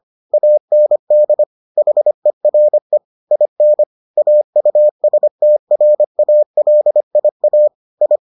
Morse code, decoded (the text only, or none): AND HERE IN AUSTRALIA I